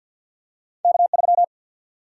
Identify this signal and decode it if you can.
Morse code, heard as a keyed tone, about 53 wpm, 700 Hz. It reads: K 3
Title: Single-Letter Single-Number 50wpm